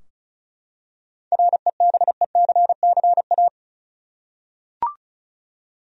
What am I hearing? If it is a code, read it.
Morse code, heard as a keyed tone, about 36 wpm, 700 Hz. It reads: REBECCA
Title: USA Common Names - 35wpm